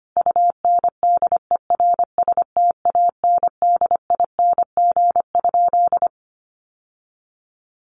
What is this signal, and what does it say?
Morse code, heard as a keyed tone, about 25 wpm, 700 Hz. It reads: UNDERSTANDING?
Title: Sentences from Top 1500 Words 25wpm